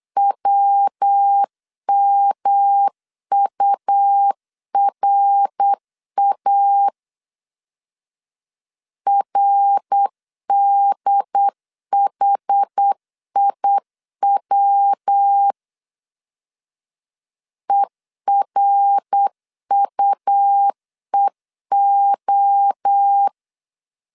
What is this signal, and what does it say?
Morse code, heard as a keyed tone, about 8 wpm, 800 Hz. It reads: WMURA RDHIW ERUEO
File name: CW / DET / 1 3 1500C S B 8wpm 4141